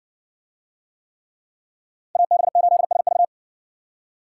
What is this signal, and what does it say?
Morse code, heard as a keyed tone, about 58 wpm, 700 Hz. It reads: AB8SV